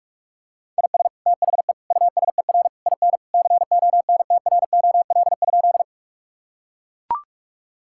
Morse code, heard as a keyed tone, about 42 wpm, 700 Hz. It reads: IS THE USER IN CONTROL?
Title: Sentences from Top 1000 Words - Review Part 5 - 45wpm